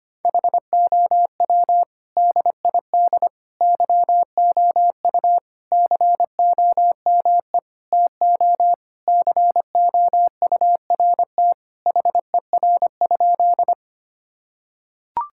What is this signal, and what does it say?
Morse code, heard as a keyed tone, about 25 wpm, 700 Hz. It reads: HOW DID YOU COME TO COURT HER?
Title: Sentences from Top 1600 Words 25wpm